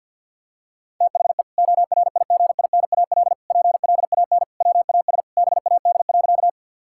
Morse code, heard as a keyed tone, about 50 wpm, 700 Hz. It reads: THE ORIGINAL PLAN WAS BAD.
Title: Sentences from Top 1000 Words - Review Part 6 - 50wpm